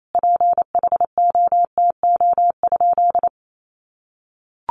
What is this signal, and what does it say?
Morse code, heard as a keyed tone, about 28 wpm, 700 Hz. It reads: PHOTO?
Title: Sentences from Top 1200 Words 28wpm